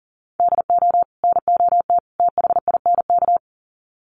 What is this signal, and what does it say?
Morse code, heard as a keyed tone, about 40 wpm, 700 Hz. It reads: DO NOT THINK
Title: Sets of 3 Words 40wpm